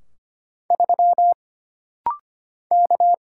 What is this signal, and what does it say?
Morse code, heard as a keyed tone, about 25 wpm, 700 Hz. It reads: 3 K
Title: Single Letter-Number - Rapid-Fire 25wpm